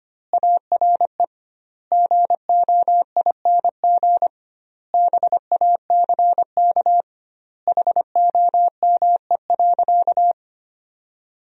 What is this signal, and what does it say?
Morse code, heard as a keyed tone, about 25 wpm, 700 Hz. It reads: ARE GOING BACK HOME.
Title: 2x Word-Spacing - Sentences from Top 500 Words - 25wpm